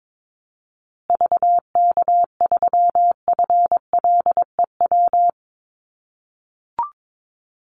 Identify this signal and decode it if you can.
Morse code, heard as a keyed tone, about 22 wpm, 690 Hz. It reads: VK3FLEW
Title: USA Call Signs Repeated 3x 22wpm